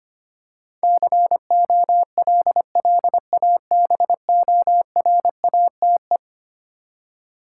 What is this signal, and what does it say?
Morse code, heard as a keyed tone, about 25 wpm, 700 Hz. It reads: COLLABORATE
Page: morse-code-ninja-practice-25wpm